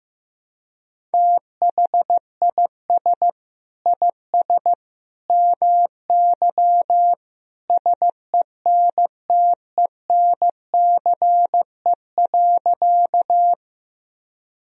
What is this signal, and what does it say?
Morse code, heard as a keyed tone, about 15 wpm, 700 Hz. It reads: THIS IS MY SENTENCE.
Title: Sentences from Top 1000 Words - Review Part 1 - 15wpm